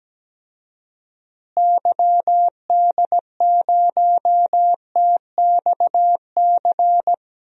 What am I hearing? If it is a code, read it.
Morse code, heard as a keyed tone, about 17 wpm, 700 Hz. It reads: YD0TXC
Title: DXCC-100 Call Signs - 6 Characters - Repeated 3x - 17wpm